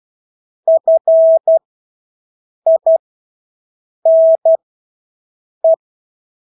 Morse code, heard as a keyed tone, about 12 wpm, spaced as for 6 wpm, 650 Hz. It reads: FINE
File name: Engelske ord 12-6 wpm